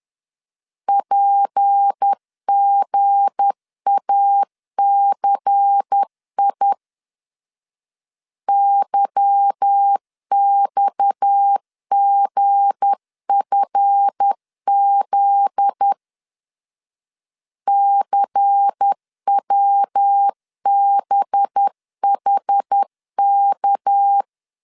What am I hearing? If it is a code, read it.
Morse code, heard as a keyed tone, about 10 wpm, 800 Hz. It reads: PGACI YXGFZ CWBHK
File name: CW / DET / 1 9 1500C S A 10wpm 4511